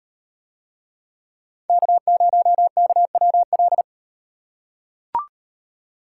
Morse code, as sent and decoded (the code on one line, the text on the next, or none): -.- ----- -.- .-- .-..
K0KWL